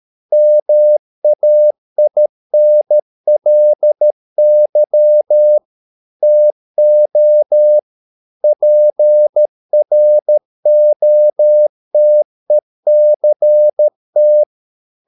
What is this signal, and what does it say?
Morse code, heard as a keyed tone, about 13 wpm, 600 Hz. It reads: MAINLY TO PROTECT
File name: Never 0013wpm